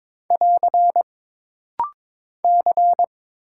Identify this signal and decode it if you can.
Morse code, heard as a keyed tone, about 22 wpm, 700 Hz. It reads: <AR> C
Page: This rapid-fire timing is geared to push your ICR (Instant Character Recognition) skills.